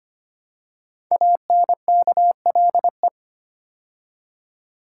Morse code, heard as a keyed tone, about 25 wpm, 700 Hz. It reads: ANKLE
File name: 5 Letter Words - First 3 Letters Given - 25wpm